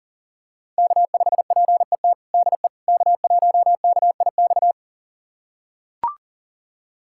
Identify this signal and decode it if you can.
Morse code, heard as a keyed tone, about 40 wpm, 700 Hz. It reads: K5PET DE K1KIX
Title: USA Callsign DE Callsign - 5 Characters - 40wpm